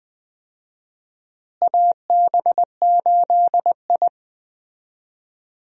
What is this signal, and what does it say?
Morse code, heard as a keyed tone, about 20 wpm, 700 Hz. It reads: AB8I
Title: USA Call Signs - 4 Characters - Repeated 2x - 20wpm